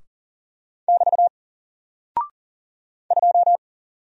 Morse code, heard as a keyed tone, about 40 wpm, 700 Hz. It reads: <BT> 2
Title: Single Character (Letter:Number:Punctuation:Prosign) Alt 40wpm